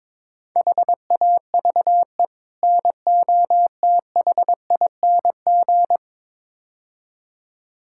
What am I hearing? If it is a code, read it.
Morse code, heard as a keyed tone, about 22 wpm, 700 Hz. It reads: HAVE NOTHING
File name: practice / 22wpm / Build A Phrase - 5-Word Phrases - 22wpm